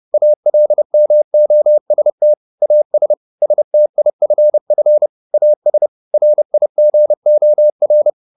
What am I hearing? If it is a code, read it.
Morse code, heard as a keyed tone, about 30 wpm, 600 Hz. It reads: ALMOST AS STIFF AS RIGOR
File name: the old man and the sea / the old man and the sea 1 30wpm